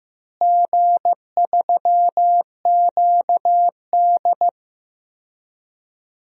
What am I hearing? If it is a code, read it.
Morse code, heard as a keyed tone, about 15 wpm, 700 Hz. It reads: G3QD
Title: DXCC-100 Call Signs - 4 Characters - Repeated 3x - 15wpm